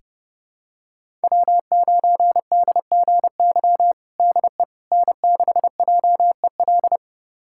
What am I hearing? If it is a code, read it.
Morse code, heard as a keyed tone, about 30 wpm, 700 Hz. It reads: W9DGY DE N6JEL
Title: USA Callsign DE Callsign - 5 Characters - 30wpm